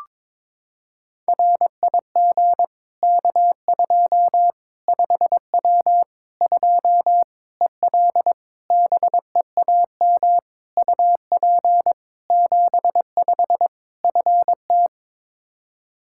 Morse code, heard as a keyed tone, about 21 wpm, 700 Hz. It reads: RIG K2 5W 2 EL BEAM UP 75 FT
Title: Practice QSO - Exchange 2 - Info - 22wpm